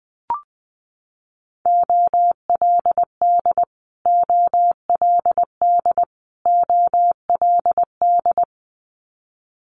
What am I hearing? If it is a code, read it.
Morse code, heard as a keyed tone, about 20 wpm, 700 Hz. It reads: OLD OLD OLD
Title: Top 300 Words Repeated 20wpm